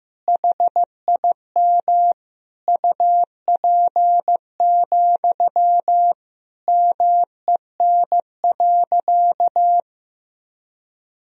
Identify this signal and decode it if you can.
Morse code, heard as a keyed tone, about 16 wpm, 700 Hz. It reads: HIM UP, MEN.
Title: Sentences from Top 1000 Words - Review Part 1 - 15wpm